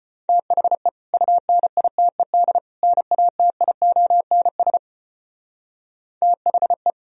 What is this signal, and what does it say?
Morse code, heard as a keyed tone, about 33 wpm, 700 Hz. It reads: THE UNITED NATIONS THE